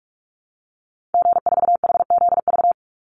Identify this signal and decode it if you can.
Morse code, heard as a keyed tone, about 44 wpm, 700 Hz. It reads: G4HZV
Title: Call Signs Repeated 3x - 45wpm